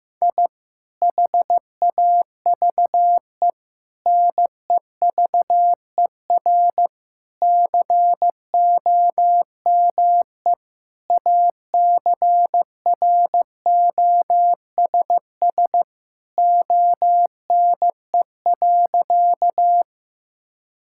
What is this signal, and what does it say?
Morse code, heard as a keyed tone, about 15 wpm, 700 Hz. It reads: I HAVE NEVER COME ACROSS ONE.
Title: Sentences from Top 1000 Words - Review Part 5 - 15wpm